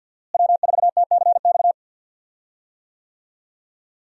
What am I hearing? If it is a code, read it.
Morse code, heard as a keyed tone, about 51 wpm, 700 Hz. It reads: W4TCX
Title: USA Call Signs - 5 Characters - Repeated 2x - 50wpm